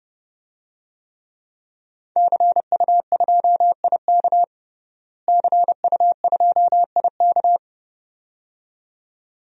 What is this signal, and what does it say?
Morse code, heard as a keyed tone, about 32 wpm, 700 Hz. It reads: CU2IK CU2IK
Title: DXCC-100 Call Signs - 5 Characters - Repeated 2x - 30wpm